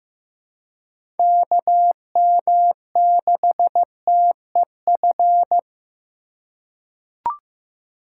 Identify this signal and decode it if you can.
Morse code, heard as a keyed tone, about 15 wpm, 700 Hz. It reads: KM6TEF
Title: USA Call Signs - 6 Characters - 3x Character Spacing - 15wpm